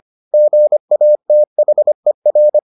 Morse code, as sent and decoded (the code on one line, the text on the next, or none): --. .- - .... .-.
GATHER